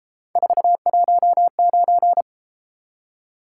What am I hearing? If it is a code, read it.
Morse code, heard as a keyed tone, about 31 wpm, 700 Hz. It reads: 419